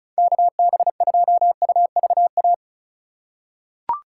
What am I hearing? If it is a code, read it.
Morse code, heard as a keyed tone, about 34 wpm, 700 Hz. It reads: KB2UVA
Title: USA Call Signs - 6 Characters - 2x Character Spacing - 35wpm